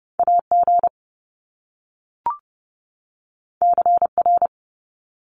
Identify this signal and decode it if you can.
Morse code, heard as a keyed tone, about 30 wpm, 700 Hz. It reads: AG CR
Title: Top Two Letter Combinations in Words 30wpm